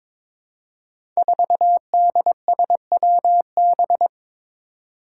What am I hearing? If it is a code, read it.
Morse code, heard as a keyed tone, about 23 wpm, 700 Hz. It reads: 4DSWB
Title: Five Challenging Letter-Number Combinations 22wpm